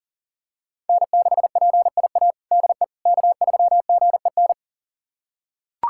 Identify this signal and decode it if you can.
Morse code, heard as a keyed tone, about 40 wpm, 690 Hz. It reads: N6PIA DE K3GEN
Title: USA Callsign DE Callsign - 5 Characters - 40wpm